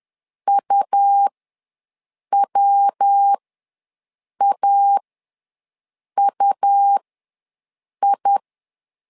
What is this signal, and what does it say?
Morse code, heard as a keyed tone, about 10 wpm, spaced as for 6 wpm, 800 Hz. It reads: UWAUI
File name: LEZIONI / 3 MWRO / 3 MW ws 10wpm 2213